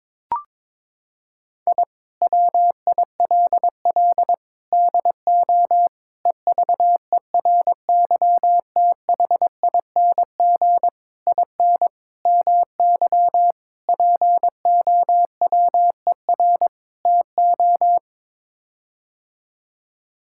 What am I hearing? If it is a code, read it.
Morse code, heard as a keyed tone, about 22 wpm, 700 Hz. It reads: I WILL DO EVERYTHING IN MY POWER TO
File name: practice / 22wpm / Sets of 8 Words - 22wpm